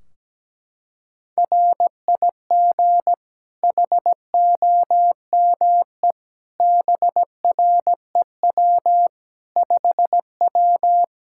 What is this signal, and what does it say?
Morse code, heard as a keyed tone, about 17 wpm, 700 Hz. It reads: RIG HOME BREW 5W
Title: K1USN SST QSO Practice 17wpm